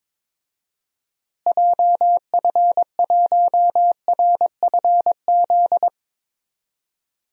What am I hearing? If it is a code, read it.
Morse code, heard as a keyed tone, about 22 wpm, 700 Hz. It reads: JF1RFZ
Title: DXCC-100 Call Signs - 6 Characters - Repeated 2x - 22wpm